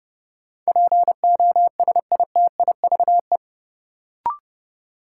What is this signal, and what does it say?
Morse code, heard as a keyed tone, about 30 wpm, 700 Hz. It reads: POSITIVE
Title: Sentences from Top 1000 Words - Review Part 5 - New Word Drill - 30wpm